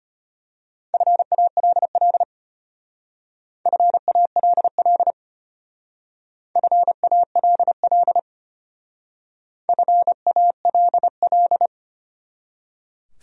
Morse code, heard as a keyed tone, about 32 wpm, 700 Hz. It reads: FALL FALL FALL FALL